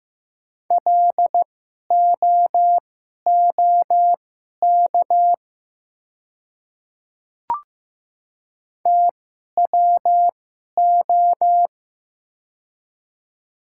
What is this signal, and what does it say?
Morse code, heard as a keyed tone, about 15 wpm, spaced as for 11 wpm, 700 Hz. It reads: LOOK TWO
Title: 2x Character Spacing - Top 100 Words - 15wpm